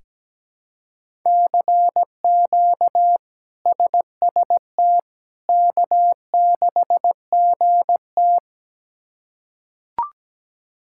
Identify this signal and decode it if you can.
Morse code, heard as a keyed tone, about 17 wpm, 700 Hz. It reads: CQ SST K6GT
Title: K1USN SST QSO Practice - 17wpm